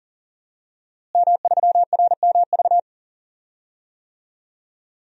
Morse code, heard as a keyed tone, about 40 wpm, 700 Hz. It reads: M3RMV
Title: DXCC-100 Call Signs - 5 Characters - Repeated 2x - 40wpm